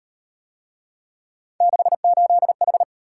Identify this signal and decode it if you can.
Morse code, heard as a keyed tone, about 34 wpm, 700 Hz. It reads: 68H